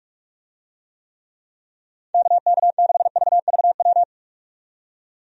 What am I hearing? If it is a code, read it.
Morse code, heard as a keyed tone, about 46 wpm, 700 Hz. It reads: KK6VVW
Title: USA Call Signs - 6 Characters - 45wpm